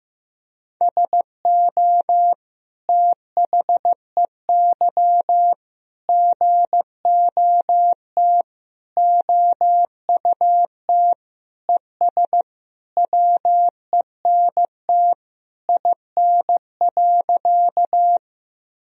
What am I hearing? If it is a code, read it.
Morse code, heard as a keyed tone, about 15 wpm, 700 Hz. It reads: SO THEY GOT OUT ES WENT IN.
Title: Sentences from Top 300 Words and Abbreviations 15wpm